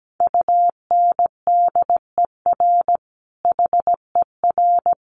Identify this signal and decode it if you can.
Morse code, heard as a keyed tone, about 17 wpm, 700 Hz. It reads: UNDER HER
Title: Sentences from Top 1300 Words 17wpm